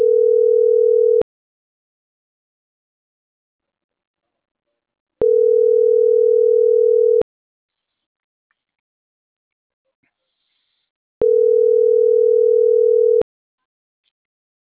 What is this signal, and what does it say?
Ringback tone (North America), 440 + 480 Hz.
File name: CENTRAL SAGOT / out rafael.zamorategui